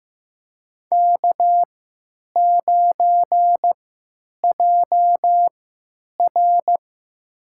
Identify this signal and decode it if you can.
Morse code, heard as a keyed tone, about 15 wpm, spaced as for 9 wpm, 700 Hz. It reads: K9JR
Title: USA Call Signs - 4 Characters - 3x Character Spacing - 15wpm